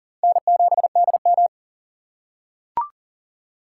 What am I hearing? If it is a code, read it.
Morse code, heard as a keyed tone, about 40 wpm, 700 Hz. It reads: N7DM
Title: USA Call Signs - 6x Character Spacing - 40wpm